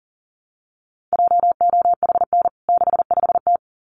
Morse code, heard as a keyed tone, about 40 wpm, 700 Hz. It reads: JOHN 65T
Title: CWT Exchange 40wpm